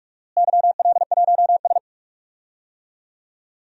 Morse code, heard as a keyed tone, about 41 wpm, 700 Hz. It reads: YL1S
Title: DXCC-100 Call Signs - 4 Characters - Repeated 2x - 45wpm